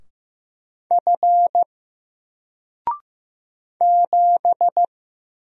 Morse code, heard as a keyed tone, about 15 wpm, 700 Hz. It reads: F 7
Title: Single Letter-Number 15wpm